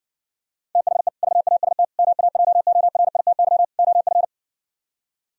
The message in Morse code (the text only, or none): THE VAST MAJORITY OF